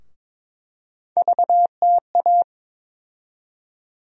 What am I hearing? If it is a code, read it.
Morse code, heard as a keyed tone, about 21 wpm, 700 Hz. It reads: VTA
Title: USA Call Sign Postfixes - 3 Letters 22wpm